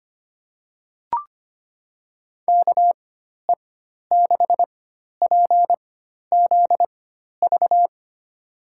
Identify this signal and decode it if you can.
Morse code, heard as a keyed tone, about 25 wpm, spaced as for 12 wpm, 700 Hz. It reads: KE6PZV
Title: USA Call Signs - 6 Characters - 4x Character Spacing - 25wpm